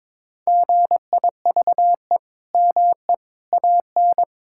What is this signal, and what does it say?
Morse code, heard as a keyed tone, about 22 wpm, 700 Hz. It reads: GIVE ME AN
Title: Sentences from Top 1000 Words - Review Part 5 - 22wpm